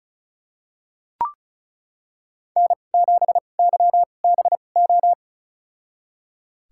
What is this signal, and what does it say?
Morse code, heard as a keyed tone, about 36 wpm, spaced as for 25 wpm, 700 Hz. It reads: N7YBO
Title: USA Call Signs - 5 Characters - 2x Character Spacing - 35wpm